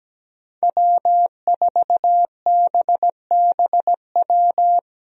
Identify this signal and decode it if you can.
Morse code, heard as a keyed tone, about 17 wpm, 700 Hz. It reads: W4BBW
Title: USA Call Signs - 5 Characters - Repeated 3x - 17wpm